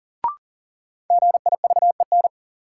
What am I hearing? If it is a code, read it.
Morse code, heard as a keyed tone, about 37 wpm, 700 Hz. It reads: GIVEN
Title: Sentences from Top 1400 Words - New Word Drill - 40wpm